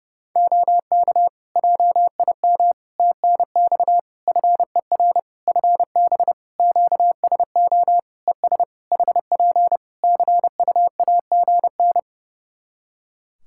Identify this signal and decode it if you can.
Morse code, heard as a keyed tone, about 30 wpm, 700 Hz. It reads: OK JIM TNX FER FB QSO ES HP CUAGN
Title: Practice QSO - Exchange 4 - Ending - 30wpm